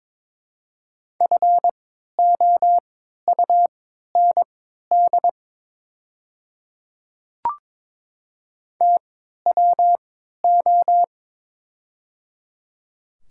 Morse code, heard as a keyed tone, about 22 wpm, spaced as for 12 wpm, 700 Hz. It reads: FOUND TWO
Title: 3x Character Spacing - Top 300 Words - 22wpm